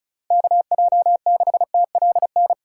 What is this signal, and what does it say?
Morse code, heard as a keyed tone, about 34 wpm, 700 Hz. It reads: KJ6TLN
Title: USA Callsign DE Callsign - 6 Characters - 35wpm